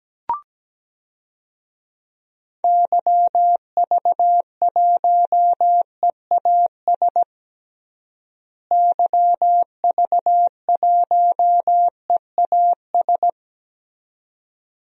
Morse code, heard as a keyed tone, about 17 wpm, 700 Hz. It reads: YV1EAS YV1EAS
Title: DXCC-100 Call Signs - 6 Characters - Repeated 2x - 17wpm